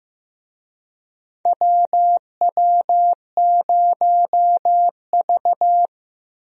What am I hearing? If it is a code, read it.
Morse code, heard as a keyed tone, about 15 wpm, 700 Hz. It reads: WW0V
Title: USA Call Signs - 4 Characters - 15wpm